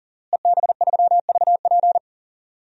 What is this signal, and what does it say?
Morse code, heard as a keyed tone, about 40 wpm, 700 Hz. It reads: EB3VP
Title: DXCC-100 Call Signs - 5 Characters - Repeated 3x - 40wpm